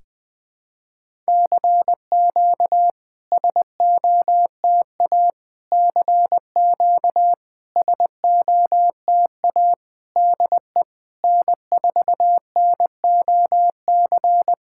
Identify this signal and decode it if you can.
Morse code, heard as a keyed tone, about 20 wpm, 690 Hz. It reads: CQ SOTA CQ SOTA DE N4NOC
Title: SOTA QSO Practice 20wpm